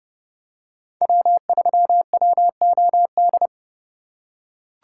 Morse code, heard as a keyed tone, about 31 wpm, 700 Hz. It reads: W3WOD
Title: USA Call Signs - 5 Characters - 2x Character Spacing - 30wpm